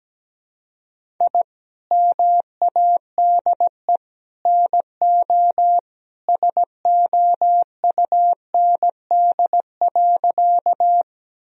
Morse code, heard as a keyed tone, about 17 wpm, 700 Hz. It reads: I MADE NO SOUND.
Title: Sentences from Top 1000 Words - Review Part 1 - 17wpm